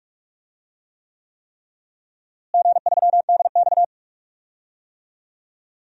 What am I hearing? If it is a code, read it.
Morse code, heard as a keyed tone, about 47 wpm, 690 Hz. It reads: G3DX
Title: DXCC-100 Call Signs - 4 Characters - Repeated 3x - 45wpm